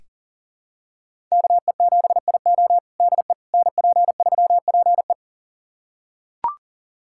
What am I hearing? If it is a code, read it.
Morse code, heard as a keyed tone, about 40 wpm, 700 Hz. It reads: KE7IO DE NP3PE
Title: USA Callsign DE Callsign - 5 Characters - 40wpm